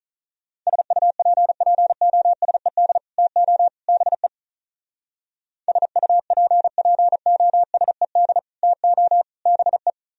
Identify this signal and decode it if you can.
Morse code, heard as a keyed tone, about 46 wpm, 700 Hz. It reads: SUPPOSED TO BE SUPPOSED TO BE